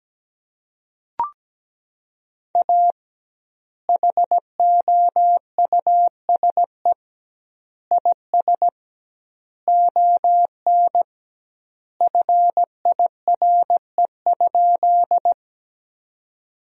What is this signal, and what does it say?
Morse code, heard as a keyed tone, about 17 wpm, 700 Hz. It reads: A HOUSE IS ON FIRE?
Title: 2x Word-Spacing - Sentences from Top 400 Words - 17wpm